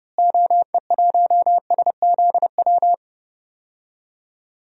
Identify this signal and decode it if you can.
Morse code, heard as a keyed tone, about 30 wpm, 700 Hz. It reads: OE1SZW
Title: DXCC-100 Call Signs - 6 Characters - Repeated 3x - 30wpm